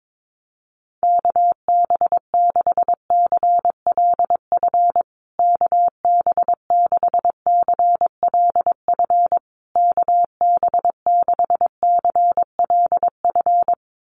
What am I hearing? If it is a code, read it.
Morse code, heard as a keyed tone, about 22 wpm, 700 Hz. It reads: KB6CLF KB6CLF KB6CLF
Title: USA Call Signs Repeated 3x 22wpm